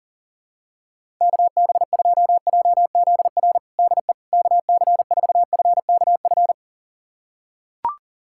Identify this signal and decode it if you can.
Morse code, heard as a keyed tone, about 40 wpm, 700 Hz. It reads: KB2JZR DE KC4FKF
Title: USA Callsign DE Callsign - 6 Characters - 40wpm